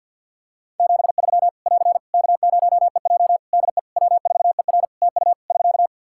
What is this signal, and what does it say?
Morse code, heard as a keyed tone, about 50 wpm, 700 Hz. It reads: 73 <AR> K0EJ DE W4ER TU <SK>
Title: Practice QSO - Exchange 4 - Ending - 50wpm